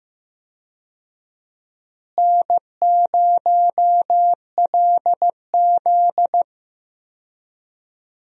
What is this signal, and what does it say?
Morse code, heard as a keyed tone, about 15 wpm, 700 Hz. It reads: N0LZ
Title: USA Call Signs - 4 Characters - Repeated 2x - 15wpm